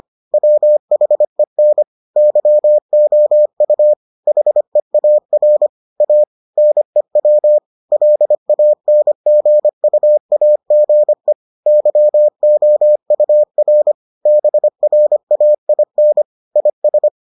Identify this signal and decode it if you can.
Morse code, heard as a keyed tone, about 25 wpm, 600 Hz. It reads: WHEN YOU HEAR A NEW LANGUAGE YOUR BRAIN IS